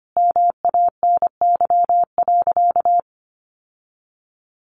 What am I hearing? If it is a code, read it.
Morse code, heard as a keyed tone, about 25 wpm, 700 Hz. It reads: MANY.
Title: Sentences from Top 1300 Words 25wpm